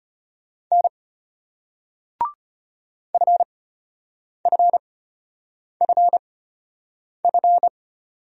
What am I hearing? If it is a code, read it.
Morse code, heard as a keyed tone, about 34 wpm, spaced as for 8 wpm, 700 Hz. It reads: N FFFF